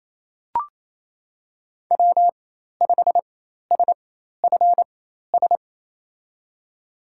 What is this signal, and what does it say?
Morse code, heard as a keyed tone, about 27 wpm, spaced as for 13 wpm, 700 Hz. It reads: W5SFS
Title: USA Call Signs - 5 Characters - 4x Character Spacing - 28wpm